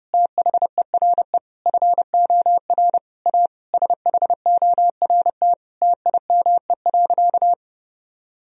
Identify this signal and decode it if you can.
Morse code, heard as a keyed tone, about 30 wpm, 700 Hz. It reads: THERE FOR A SHORT TIME.
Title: Sentences from Top 1000 Words - Review Part 2 - 30wpm